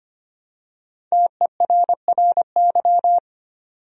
Morse code, heard as a keyed tone, about 25 wpm, 700 Hz. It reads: TERRY
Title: USA Common Names - Max 5 Letters - 25wpm